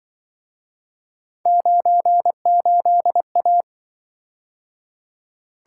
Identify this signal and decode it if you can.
Morse code, heard as a keyed tone, about 25 wpm, 700 Hz. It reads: 98A